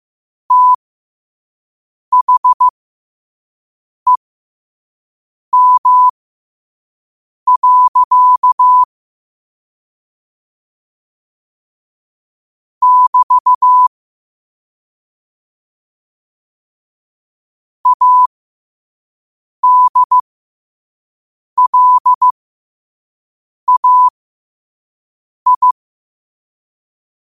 Morse code, heard as a keyed tone, about 15 wpm, spaced as for 5 wpm, 1000 Hz. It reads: THEM. <BT> ADLAI
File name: qotdfiles / 5wpm